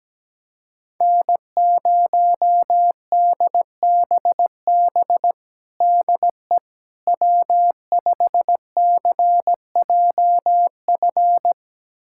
Morse code, heard as a keyed tone, about 17 wpm, 700 Hz. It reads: N0DBB DE W5CJF
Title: USA Callsign DE Callsign - 5 Characters - 17wpm